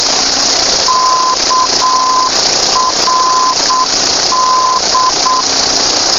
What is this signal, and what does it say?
Morse code, heard as a keyed tone, about 9 wpm, 1020 Hz, with noise. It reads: KRD
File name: uploader / krd 117.05 lep 116.9